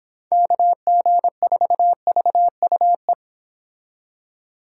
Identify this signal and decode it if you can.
Morse code, heard as a keyed tone, about 29 wpm, 700 Hz. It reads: KG4VUE